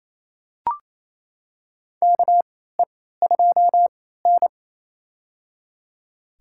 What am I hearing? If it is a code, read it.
Morse code, heard as a keyed tone, about 28 wpm, spaced as for 16 wpm, 700 Hz. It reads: KE2N
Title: USA Call Signs - 3x Character Spacing - 28wpm